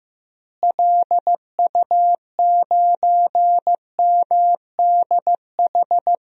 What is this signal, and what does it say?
Morse code, heard as a keyed tone, about 16 wpm, 700 Hz. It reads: LU9MDH
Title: DXCC-100 Call Signs - 6 Characters - Repeated 2x - 15wpm